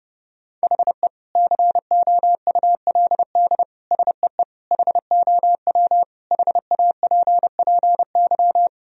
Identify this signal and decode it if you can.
Morse code, heard as a keyed tone, about 30 wpm, 700 Hz. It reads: HE COULD SEE HOW HAPPY
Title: Sentences from Top 1000 Words - Review Part 3 - 30wpm